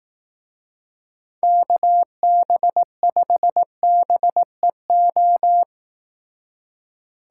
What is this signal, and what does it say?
Morse code, heard as a keyed tone, about 18 wpm, 700 Hz. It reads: KB5BEO
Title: Speed Racing - USA 2x3 Call Signs 15wpm